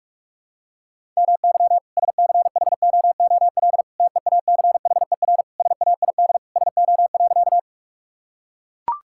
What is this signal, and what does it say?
Morse code, heard as a keyed tone, about 42 wpm, 690 Hz. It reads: MY SCHOOL TEACHER SAID SO.
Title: Sentences from Top 1000 Words - Review Part 6 - 45wpm